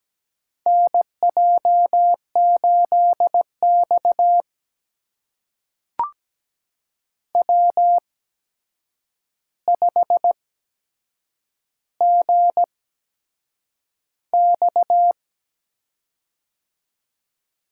Morse code, heard as a keyed tone, about 17 wpm, spaced as for 7 wpm, 700 Hz. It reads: NJ8X W5GX
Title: USA Call Signs - 4 Characters - 8x Character Spacing - 17wpm